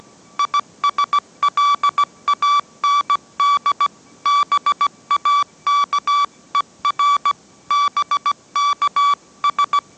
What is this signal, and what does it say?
Morse code, heard as a keyed tone, about 17 wpm, 1150 Hz. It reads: ISLAND BAKER BKS
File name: mrx / mrx080 18wpm